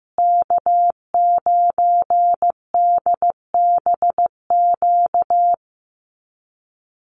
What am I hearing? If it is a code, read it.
Morse code, heard as a keyed tone, about 15 wpm, 700 Hz. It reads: K9DBQ
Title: USA Call Signs Repeated 3x 15wpm